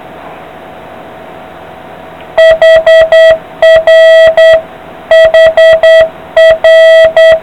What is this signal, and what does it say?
Morse code, heard as a keyed tone, about 9 wpm, 660 Hz. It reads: HR HR